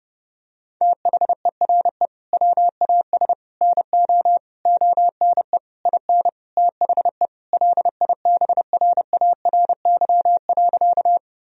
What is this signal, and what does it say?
Morse code, heard as a keyed tone, about 32 wpm, 700 Hz. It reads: THERE WAS NO ONE IN THE LIBRARY.
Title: Sentences from Top 1000 Words - Review Part 4 - 30wpm